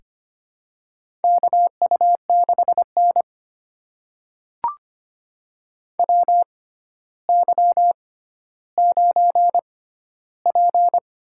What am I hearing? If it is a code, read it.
Morse code, heard as a keyed tone, about 25 wpm, spaced as for 13 wpm, 700 Hz. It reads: KU6N WY9P
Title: USA Call Signs - 4 Characters - 6x Character Spacing - 25wpm